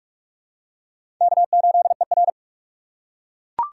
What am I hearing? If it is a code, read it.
Morse code, heard as a keyed tone, about 41 wpm, 700 Hz. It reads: K8ER
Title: USA Call Signs - 4 Characters - 2x Character Spacing - 45wpm